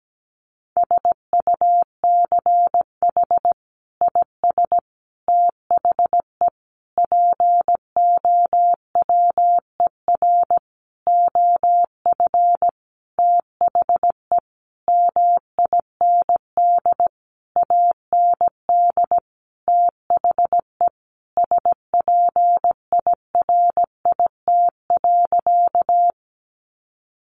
Morse code, heard as a keyed tone, about 17 wpm, 700 Hz. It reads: SUCH IS THE POWER OF THE MIND AND THE SPIRIT.
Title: Sentences from Top 1400 Words 17wpm